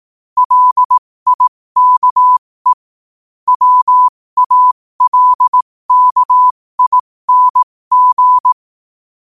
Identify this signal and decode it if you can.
Morse code, heard as a keyed tone, about 18 wpm, spaced as for 15 wpm, 1000 Hz. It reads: LIKE WALKING